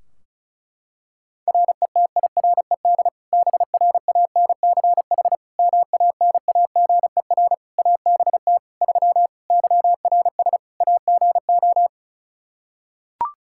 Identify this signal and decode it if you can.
Morse code, heard as a keyed tone, about 36 wpm, 700 Hz. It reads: RETIRED BRANCH MANAGER ABT 3 YRS AGO
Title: Practice QSO - Exchange 3 - More Info - 35wpm